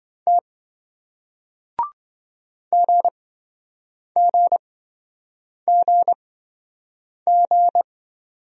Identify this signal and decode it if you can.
Morse code, heard as a keyed tone, about 27 wpm, spaced as for 7 wpm, 700 Hz. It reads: T GGGG